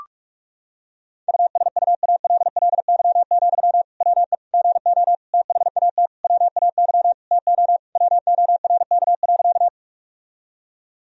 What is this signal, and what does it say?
Morse code, heard as a keyed tone, about 46 wpm, 700 Hz. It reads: USUALLY, WE GO THAT WAY TO WORK.
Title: Sentences from Top 1000 Words - Review Part 3 - 45wpm